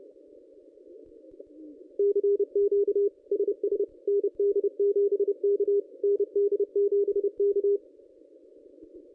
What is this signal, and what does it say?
Morse code, heard as a keyed tone, about 30 wpm, 380 Hz. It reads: CQ SS ND7K ND7K
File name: contests / ss-cw-k5tr-op-n5rz -1200